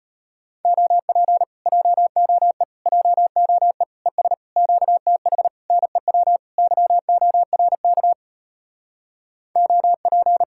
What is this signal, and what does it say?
Morse code, heard as a keyed tone, about 33 wpm, 700 Hz. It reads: OP JOE JOE ES QTH NEW YORK OP